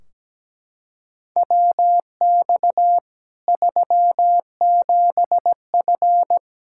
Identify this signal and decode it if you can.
Morse code, heard as a keyed tone, about 17 wpm, 700 Hz. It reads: WX 37F
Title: Practice QSO - Exchange 2 - Info - 17wpm